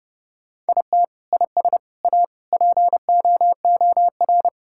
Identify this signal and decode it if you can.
Morse code, heard as a keyed tone, about 30 wpm, 700 Hz. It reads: IT IS A POOR
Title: Sentences from Top 1000 Words - Review Part 5 - 30wpm